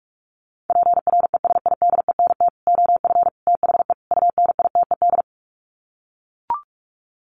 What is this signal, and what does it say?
Morse code, heard as a keyed tone, about 44 wpm, 700 Hz. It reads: PRESIDENT OF THE UNITED
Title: Sets of 4 Words 45wpm